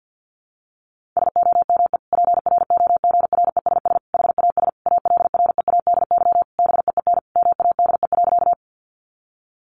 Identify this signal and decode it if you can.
Morse code, heard as a keyed tone, about 50 wpm, 700 Hz. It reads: SOME PROGRESS HAS ALREADY BEEN MADE.
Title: Sentences from Top 1400 Words 50wpm